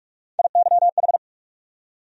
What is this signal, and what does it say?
Morse code, heard as a keyed tone, about 42 wpm, 700 Hz. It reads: IYH
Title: USA Call Sign Postfixes - 3 Letters 45wpm